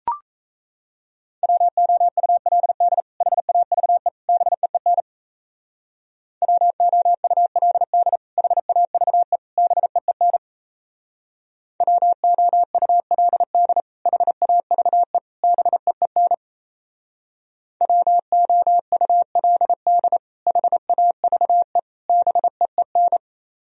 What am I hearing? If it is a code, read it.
Morse code, heard as a keyed tone, about 34 wpm, 700 Hz. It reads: WOULD HAVE BEEN WOULD HAVE BEEN WOULD HAVE BEEN WOULD HAVE BEEN
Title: Speed-Racing - Sets of 3 Words 28wpm